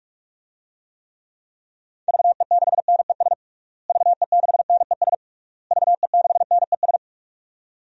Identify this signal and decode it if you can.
Morse code, heard as a keyed tone, about 46 wpm, 700 Hz. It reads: VE6NES VE6NES VE6NES
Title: DXCC-100 Call Signs - 6 Characters - Repeated 3x - 45wpm